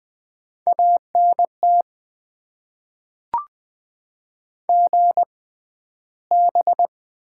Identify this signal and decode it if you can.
Morse code, heard as a keyed tone, about 20 wpm, spaced as for 12 wpm, 700 Hz. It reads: ANT GB
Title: Top 90 QSO Elements - 6x Character Spacing - 20wpm